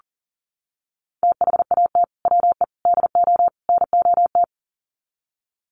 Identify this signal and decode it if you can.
Morse code, heard as a keyed tone, about 40 wpm, 700 Hz. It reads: THAT WE DO NOT
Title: Sets of 4 Words 40wpm